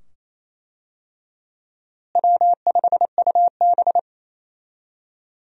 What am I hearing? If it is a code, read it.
Morse code, heard as a keyed tone, about 28 wpm, 700 Hz. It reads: W5UB
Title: USA Call Signs - 4 Characters - Repeated 2x - 28wpm